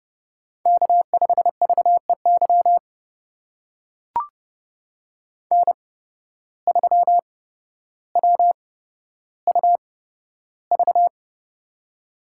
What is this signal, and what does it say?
Morse code, heard as a keyed tone, about 30 wpm, spaced as for 13 wpm, 700 Hz. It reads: K5VEY N3WUV
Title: USA Call Signs - 5 Characters - 8x Character Spacing - 30wpm